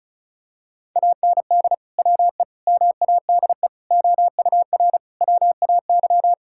Morse code, heard as a keyed tone, about 36 wpm, 700 Hz. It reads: AND WE MADE OUR WAY
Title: Sentences from Top 1600 Words 35wpm